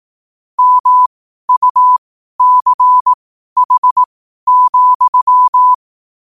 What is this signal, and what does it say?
Morse code, heard as a keyed tone, about 18 wpm, spaced as for 12 wpm, 1000 Hz. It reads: MUCH,
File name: qotdfiles / 12wpm